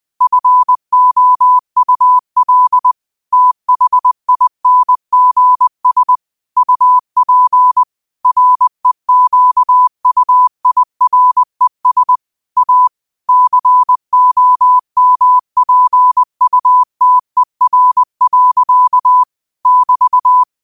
Morse code, heard as a keyed tone, about 20 wpm, 1000 Hz. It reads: FOUL THINGS UP REQUIRES A COMPUTER. <BT>